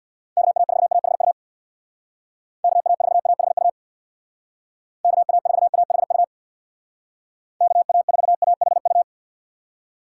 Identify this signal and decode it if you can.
Morse code, heard as a keyed tone, about 64 wpm, 700 Hz. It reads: KA4AHU KA4AHU KA4AHU KA4AHU